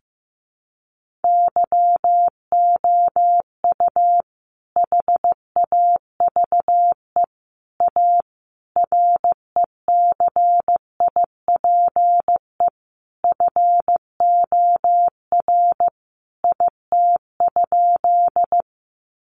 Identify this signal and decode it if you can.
Morse code, heard as a keyed tone, about 14 wpm, 700 Hz. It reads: YOU HAVE A RECIPE FOR IT?
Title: Sentences from Top 1200 Words 15wpm